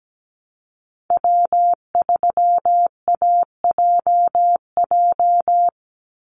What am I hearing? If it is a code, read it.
Morse code, heard as a keyed tone, about 17 wpm, 700 Hz. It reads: W3AJJ
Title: USA Call Signs 17wpm